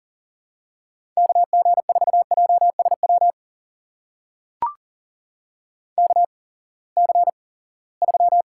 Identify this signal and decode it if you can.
Morse code, heard as a keyed tone, about 39 wpm, spaced as for 23 wpm, 700 Hz. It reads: KG4JSW KC3
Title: USA Call Signs - 6 Characters - 8x Character Spacing - 40wpm